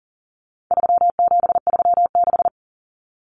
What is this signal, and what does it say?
Morse code, heard as a keyed tone, about 40 wpm, 700 Hz. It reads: 3736
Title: Numbers - 4-Digit 40wpm